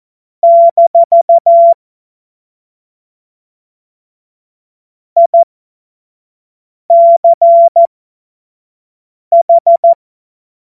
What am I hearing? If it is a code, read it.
Morse code, heard as a keyed tone, about 14 wpm, spaced as for 5 wpm, 680 Hz. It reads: - ICH
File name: Morsekurs / Witze / Witz 11. April 2023 25 F70